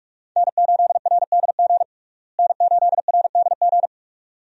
Morse code, heard as a keyed tone, about 46 wpm, 700 Hz. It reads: N8RDG N8RDG
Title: USA Call Signs - 5 Characters - Repeated 2x - 45wpm